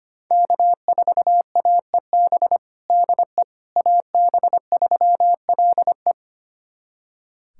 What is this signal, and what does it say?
Morse code, heard as a keyed tone, about 25 wpm, 700 Hz. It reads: K4AEB DE AB3LE
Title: USA Callsign DE Callsign - 5 Characters - 25wpm